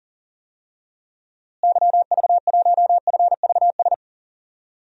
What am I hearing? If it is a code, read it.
Morse code, heard as a keyed tone, about 40 wpm, 700 Hz. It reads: YV1FVS
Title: DXCC-100 Call Signs - 6 Characters - Repeated 2x - 40wpm